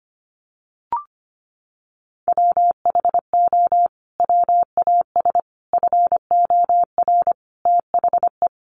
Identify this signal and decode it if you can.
Morse code, heard as a keyed tone, about 25 wpm, 700 Hz. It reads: WHO WAS FOR THE
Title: Sentences from Top 1400 Words 25wpm